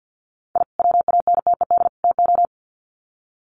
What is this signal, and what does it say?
Morse code, heard as a keyed tone, about 50 wpm, 700 Hz. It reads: I WANTED TO
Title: Sets of 3 Words 50wpm